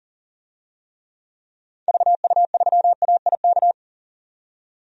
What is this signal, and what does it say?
Morse code, heard as a keyed tone, about 40 wpm, 700 Hz. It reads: VU3AIK
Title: DXCC-100 Call Signs - 6 Characters - 40wpm